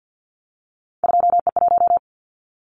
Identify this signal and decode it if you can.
Morse code, heard as a keyed tone, about 50 wpm, 700 Hz. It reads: 2E1
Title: Callsign Prefixes for Top 25 DXCC - 50wpm